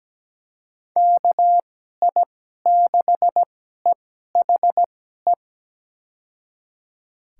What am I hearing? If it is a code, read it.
Morse code, heard as a keyed tone, about 17 wpm, spaced as for 12 wpm, 700 Hz. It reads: KI6EHE
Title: USA Call Signs - 6 Characters - 2x Character Spacing - 17wpm